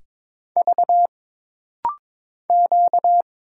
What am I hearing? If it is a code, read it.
Morse code, heard as a keyed tone, about 22 wpm, 700 Hz. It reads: V Q
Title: Single Letters - Rapid-Fire 22wpm